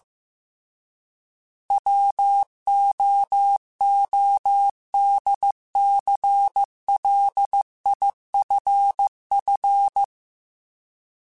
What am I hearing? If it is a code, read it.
Morse code, heard as a keyed tone, about 15 wpm, 800 Hz. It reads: WOODCLIFF